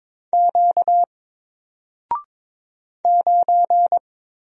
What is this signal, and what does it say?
Morse code, heard as a keyed tone, about 22 wpm, 700 Hz. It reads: Q 9
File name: Single-Letter-Number-22wpm